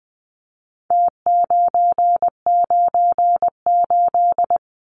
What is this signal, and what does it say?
Morse code, heard as a keyed tone, about 20 wpm, 700 Hz. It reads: T998
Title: Numbers - Contest Style - 4-Digit 20wpm